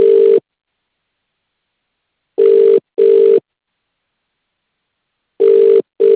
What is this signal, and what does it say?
Ringback tone (UK), 400 + 450 Hz.